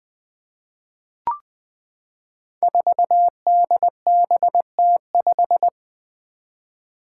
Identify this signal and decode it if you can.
Morse code, heard as a keyed tone, about 20 wpm, 700 Hz. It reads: 4DBT5
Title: Five Challenging Letter-Number Combinations 20wpm